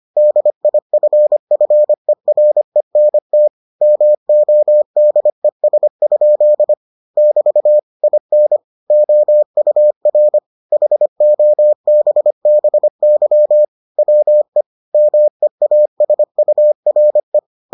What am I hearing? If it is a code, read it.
Morse code, heard as a keyed tone, about 25 wpm, 600 Hz. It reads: DIFFERENT MODES? <BT> IN OUR HOBBY WE MEASURE